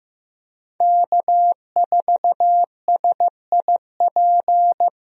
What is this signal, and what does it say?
Morse code, heard as a keyed tone, about 16 wpm, 700 Hz. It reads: K4SIP
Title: USA Call Signs - 5 Characters - Repeated 2x - 15wpm